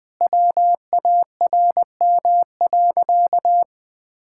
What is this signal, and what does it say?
Morse code, heard as a keyed tone, about 21 wpm, 700 Hz. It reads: WARM.
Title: Sentences from Top 1000 Words - Review Part 5 - 20wpm